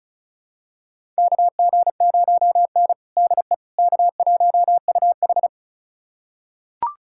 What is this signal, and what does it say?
Morse code, heard as a keyed tone, about 36 wpm, 700 Hz. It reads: KG0N DE K1UH
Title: USA Callsign DE Callsign - 4 Characters - 35wpm